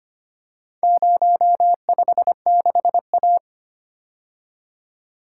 Morse code, heard as a keyed tone, about 25 wpm, 700 Hz. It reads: 056A
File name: practice / 25wpm / Four Letter-Number Combinations - 25wpm